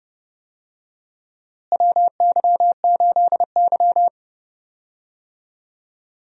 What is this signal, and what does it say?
Morse code, heard as a keyed tone, about 30 wpm, 700 Hz. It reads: WY8Y
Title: USA Call Signs - 4 Characters - Repeated 3x - 30wpm